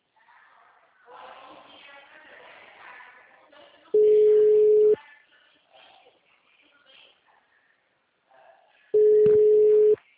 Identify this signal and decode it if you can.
Ringback tone (Europe, 425 Hz), with speech.